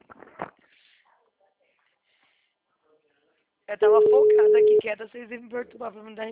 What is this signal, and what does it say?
Ringback tone (Europe, 425 Hz), with speech.